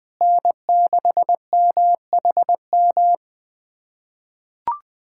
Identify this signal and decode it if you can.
Morse code, heard as a keyed tone, about 20 wpm, 700 Hz. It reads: N6MHM
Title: USA Callsign DE Callsign - 5 Characters - 20wpm